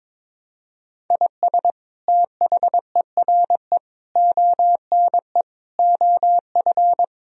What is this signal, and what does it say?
Morse code, heard as a keyed tone, about 22 wpm, 700 Hz. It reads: IS THERE ONE OF
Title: Sentences from Top 1000 Words - Review Part 2 - 22wpm